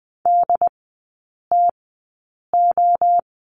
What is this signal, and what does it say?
Morse code, heard as a keyed tone, about 20 wpm, spaced as for 8 wpm, 690 Hz. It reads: DTO
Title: 081 Lesson 14e - Number 2 - Three Characters 20wpm